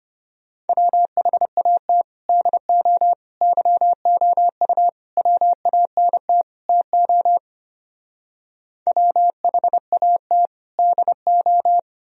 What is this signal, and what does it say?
Morse code, heard as a keyed tone, about 31 wpm, 700 Hz. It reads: WHAT DO YOU WANT TO WHAT DO
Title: Speed-Racing - Sets of 5 Words 25wpm